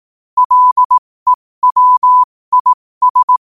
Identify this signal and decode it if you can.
Morse code, heard as a keyed tone, about 18 wpm, spaced as for 15 wpm, 1000 Hz. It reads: LEWIS
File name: qotdfiles / 15wpm